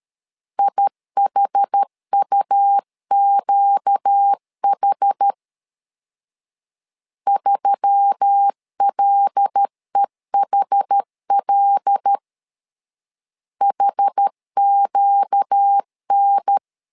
Morse code, heard as a keyed tone, about 13 wpm, 800 Hz. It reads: IHUQH 3LEHL HQN
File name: LEZIONI / 5 LBVC / 5 L 12wpm 1913